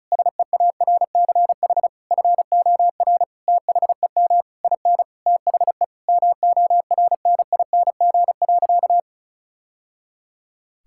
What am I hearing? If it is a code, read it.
Morse code, heard as a keyed tone, about 36 wpm, 700 Hz. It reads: SEARCH FOR THEM IN THE MORNING.
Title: Sentences from Top 1000 Words - Review Part 5 - 35wpm